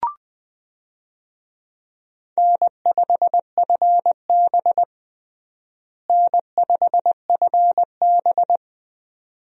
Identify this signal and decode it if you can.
Morse code, heard as a keyed tone, about 20 wpm, 700 Hz. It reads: N5FB N5FB
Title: USA Call Signs - 4 Characters - Repeated 2x - 20wpm